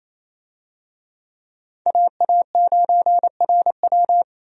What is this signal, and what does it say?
Morse code, heard as a keyed tone, about 28 wpm, 700 Hz. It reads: AA9RW
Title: USA Call Signs - 5 Characters - Repeated 3x - 28wpm